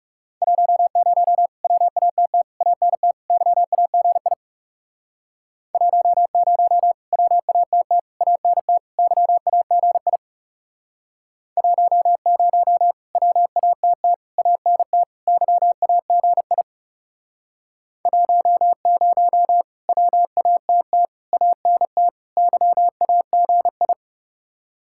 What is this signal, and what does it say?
Morse code, heard as a keyed tone, about 39 wpm, 700 Hz. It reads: 10 WATT ANT YAGI 10 WATT ANT YAGI 10 WATT ANT YAGI 10 WATT ANT YAGI